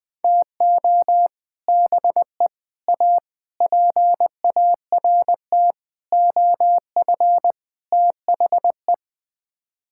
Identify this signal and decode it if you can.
Morse code, heard as a keyed tone, about 20 wpm, 700 Hz. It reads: TO BE A PART OF THE
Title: Sets of 6 Words - 20wpm